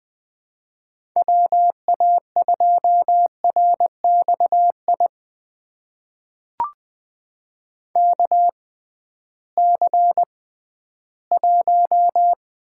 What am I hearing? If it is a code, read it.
Morse code, heard as a keyed tone, about 21 wpm, spaced as for 13 wpm, 700 Hz. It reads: WA2RXI KC1
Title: USA Call Signs - 6 Characters - 6x Character Spacing - 20wpm